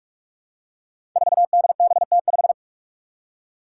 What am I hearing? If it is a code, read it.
Morse code, heard as a keyed tone, about 46 wpm, 700 Hz. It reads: 4DBT5
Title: Five Challenging Letter-Number Combinations 45wpm